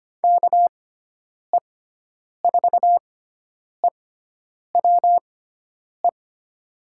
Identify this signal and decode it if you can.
Morse code, heard as a keyed tone, about 25 wpm, spaced as for 9 wpm, 700 Hz. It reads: KE4EWE
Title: USA Call Signs - 6 Characters - 6x Character Spacing - 25wpm